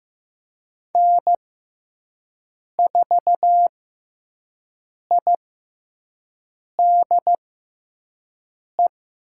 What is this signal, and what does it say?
Morse code, heard as a keyed tone, about 15 wpm, spaced as for 5 wpm, 700 Hz. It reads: N4IDE
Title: USA Call Signs - 5 Characters - 6x Character Spacing - 15wpm